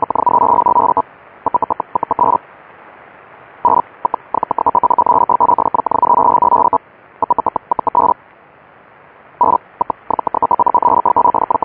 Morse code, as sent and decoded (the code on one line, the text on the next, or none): none